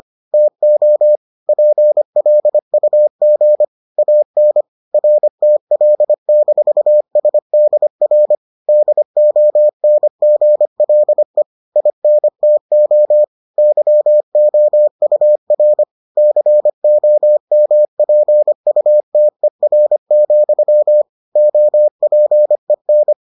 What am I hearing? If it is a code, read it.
Morse code, heard as a keyed tone, about 25 wpm, 600 Hz. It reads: TO PLUG AN RTL-SDR DONGLE INTO YOUR COMPUTER, OPEN